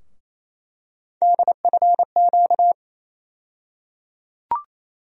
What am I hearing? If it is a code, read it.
Morse code, heard as a keyed tone, about 28 wpm, 700 Hz. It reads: DFQ
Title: USA Call Sign Postfixes - 3 Letters 28wpm